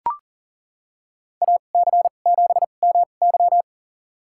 Morse code, heard as a keyed tone, about 38 wpm, spaced as for 28 wpm, 700 Hz. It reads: AC7MY
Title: USA Call Signs - 2x Character Spacing - 40wpm